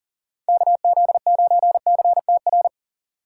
Morse code, heard as a keyed tone, about 40 wpm, 700 Hz. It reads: KZ9CTR
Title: USA Call Signs - 4x Character Spacing - 40wpm